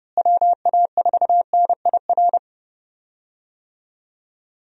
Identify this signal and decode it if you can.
Morse code, heard as a keyed tone, about 31 wpm, 700 Hz. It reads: WA4NIR
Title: USA Call Signs - 6 Characters - Repeated 3x - 30wpm